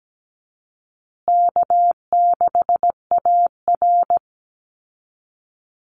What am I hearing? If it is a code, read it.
Morse code, heard as a keyed tone, about 17 wpm, 700 Hz. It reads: K6AR
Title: Easy Call Signs Repeated 3x - 17wpm